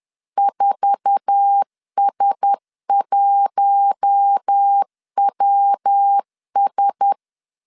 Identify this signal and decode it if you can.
Morse code, heard as a keyed tone, about 10 wpm, 800 Hz. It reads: 4S1WS
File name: CW / DET / 1 9 1500C S A 10wpm 4511